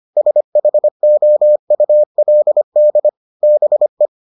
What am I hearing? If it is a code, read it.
Morse code, heard as a keyed tone, about 25 wpm, 600 Hz. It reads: SHOULD BE